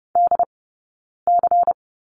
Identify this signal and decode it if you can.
Morse code, heard as a keyed tone, about 30 wpm, 700 Hz. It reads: D C
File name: Lesson 16d - Letter c - Two Characters 30wpm